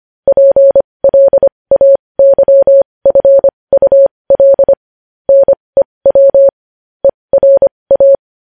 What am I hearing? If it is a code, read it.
Morse code, heard as a keyed tone, about 25 wpm, spaced as for 20 wpm, 550 Hz. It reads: PLAYFUL NEW ERA
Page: News headlines in morse code at 20 wpm